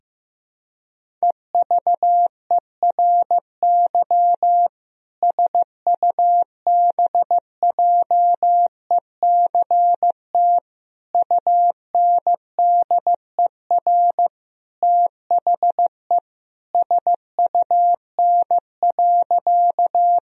Morse code, heard as a keyed tone, about 15 wpm, 700 Hz. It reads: EVERY SUBJECT UNDER THE SUN.
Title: Sentences from Top 1000 Words - Review Part 1 - 15wpm